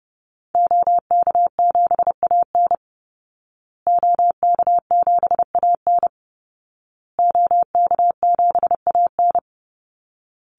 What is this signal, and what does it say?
Morse code, heard as a keyed tone, about 30 wpm, 700 Hz. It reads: OK7AN OK7AN OK7AN
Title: Call Signs Repeated 2x - 30wpm